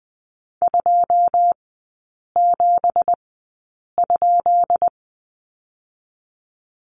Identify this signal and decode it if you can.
Morse code, heard as a keyed tone, about 20 wpm, spaced as for 8 wpm, 700 Hz. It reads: 27?
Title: 166 Lesson 27e - Number 7 - Three Characters 20wpm